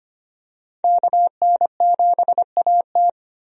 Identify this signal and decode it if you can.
Morse code, heard as a keyed tone, about 25 wpm, 700 Hz. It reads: KN7AT
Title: USA Call Signs - 5 Characters - 25wpm